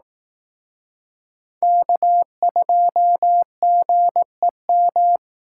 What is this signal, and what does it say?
Morse code, heard as a keyed tone, about 18 wpm, 700 Hz. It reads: K2GEM
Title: Speed Racing - USA 1x3 Call Signs 15wpm